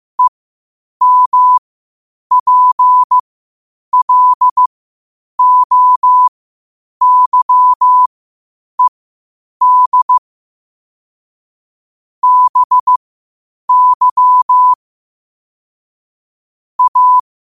Morse code, heard as a keyed tone, about 15 wpm, spaced as for 8 wpm, 1000 Hz. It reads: EMPLOYED BY A
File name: qotdfiles / 8wpm